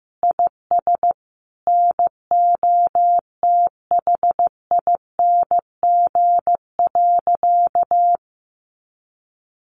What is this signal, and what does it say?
Morse code, heard as a keyed tone, about 15 wpm, 700 Hz. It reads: IS NOTHING.
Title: Sentences from Top 500 Words 15wpm